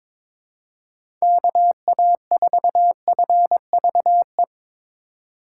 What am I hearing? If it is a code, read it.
Morse code, heard as a keyed tone, about 22 wpm, 700 Hz. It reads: KA4FVE
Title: USA Call Signs - 6 Characters - 2x Character Spacing - 22wpm